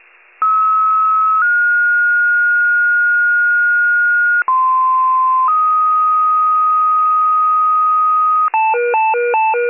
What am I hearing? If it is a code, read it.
Two-tone page, heard as a one-second tone, then a three-second tone, about 1290 then 1430 Hz, about 1040 then 1250 Hz, with noise. It is heard from 0.4 to 8.5 s.